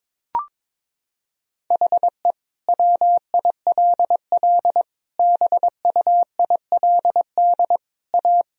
Morse code, heard as a keyed tone, about 22 wpm, 700 Hz. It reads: HE WILL BUILD A
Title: Sentences from Top 1000 Words - Review Part 1 - 22wpm